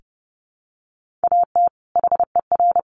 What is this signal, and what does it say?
Morse code, heard as a keyed tone, about 30 wpm, 700 Hz. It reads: AT HER
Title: Sentences from Top 1500 Words 30wpm